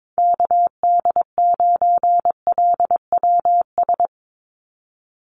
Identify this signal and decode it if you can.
Morse code, heard as a keyed tone, about 22 wpm, 700 Hz. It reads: KD9LWS
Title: USA Call Signs 22wpm